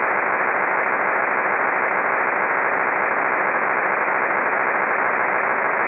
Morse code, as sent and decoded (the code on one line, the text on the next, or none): none